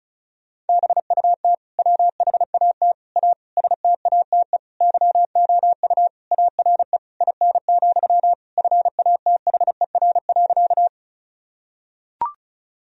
Morse code, heard as a keyed tone, about 36 wpm, 700 Hz. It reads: BUT WHAT A STATE YOU ARE IN, FATHER.
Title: Sentences from Top 1000 Words - Review Part 2 - 35wpm